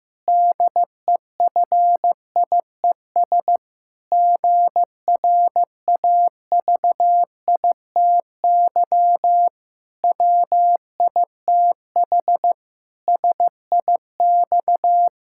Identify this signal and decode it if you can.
Morse code, heard as a keyed tone, about 15 wpm, 700 Hz. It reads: DEFIES GRAVITY WITH SIX
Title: Pangram Challenge v2 - 15wpm